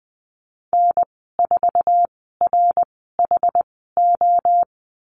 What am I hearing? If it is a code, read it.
Morse code, heard as a keyed tone, about 19 wpm, spaced as for 14 wpm, 700 Hz. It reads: N4RHO
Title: 128 Lesson 21g - Number 6 - Call Signs 20wpm